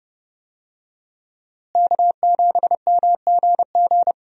Morse code, heard as a keyed tone, about 31 wpm, 700 Hz. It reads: K7MGG
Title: USA Call Signs - 5 Characters - Repeated 3x - 30wpm